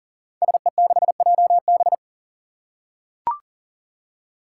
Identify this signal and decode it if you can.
Morse code, heard as a keyed tone, about 38 wpm, 700 Hz. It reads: SE6JB
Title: Five Challenging Letter-Number Combinations 40wpm